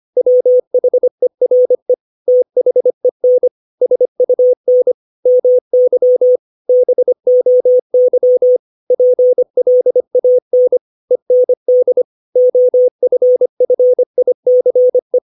Morse code, heard as a keyed tone, about 25 wpm, 500 Hz. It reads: WHERE THEN SUN MY BOY PLAN END OFFICE
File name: 632 words / 632 Words 25WPM